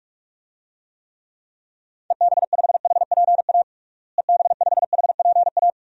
Morse code, heard as a keyed tone, about 47 wpm, 700 Hz. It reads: EB5HPA EB5HPA
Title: DXCC-100 Call Signs - 6 Characters - Repeated 3x - 45wpm